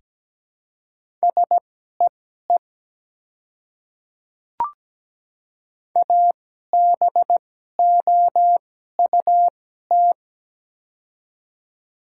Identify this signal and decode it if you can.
Morse code, heard as a keyed tone, about 17 wpm, spaced as for 12 wpm, 700 Hz. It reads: SEE ABOUT
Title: 2x Character Spacing - Top 100 Words - 17wpm